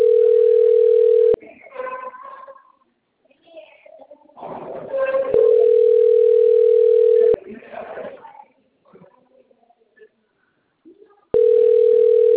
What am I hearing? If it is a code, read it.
Ringback tone (North America), 440 + 480 Hz, with speech.
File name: rg-1600-unknown- - - .57005